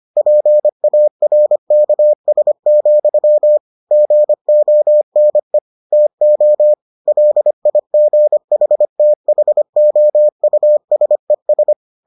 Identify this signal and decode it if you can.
Morse code, heard as a keyed tone, about 25 wpm, 600 Hz. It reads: PARKS, GONE TO LIGHTHOUSES